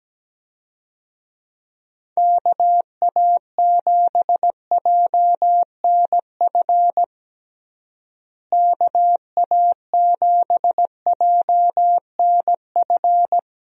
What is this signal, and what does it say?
Morse code, heard as a keyed tone, about 17 wpm, 700 Hz. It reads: KA7JNF KA7JNF
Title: USA Call Signs - 6 Characters - Repeated 2x - 17wpm